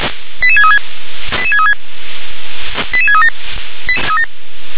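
Five-tone selective call (ZVEI), heard as quick runs of five tones, tones of about 70 ms. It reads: 88636 88636 88637
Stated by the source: ZVEI five-tone call received on an SDR.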